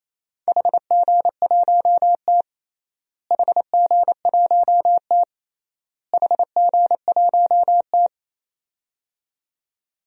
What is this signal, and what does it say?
Morse code, heard as a keyed tone, about 28 wpm, 700 Hz. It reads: HG1T HG1T HG1T
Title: DXCC-100 Call Signs - 4 Characters - Repeated 3x - 28wpm